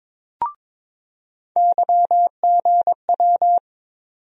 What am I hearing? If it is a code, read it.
Morse code, heard as a keyed tone, about 22 wpm, 700 Hz. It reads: YGW
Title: USA Call Sign Postfixes - 3 Letters 22wpm